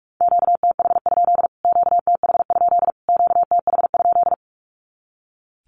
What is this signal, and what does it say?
Morse code, heard as a keyed tone, about 45 wpm, 700 Hz. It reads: QTH? QTH? QTH?
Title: Q-Codes Repeated 3x 45wpm